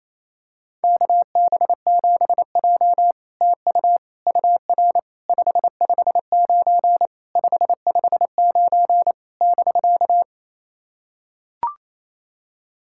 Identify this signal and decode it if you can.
Morse code, heard as a keyed tone, about 29 wpm, 690 Hz. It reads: KB7J TU UR 559 559 <BK>
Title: SOTA QSO Practice 28wpm